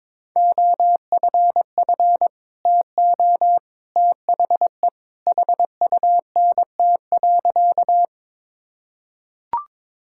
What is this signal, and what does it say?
Morse code, heard as a keyed tone, about 22 wpm, 700 Hz. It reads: OFF TO THE HUNT.
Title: Sentences from Top 1600 Words 22wpm